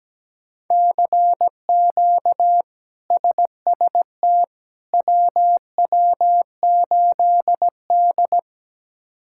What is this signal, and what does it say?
Morse code, heard as a keyed tone, about 17 wpm, 700 Hz. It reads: CQ SST WW8D
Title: K1USN SST QSO Practice - 17wpm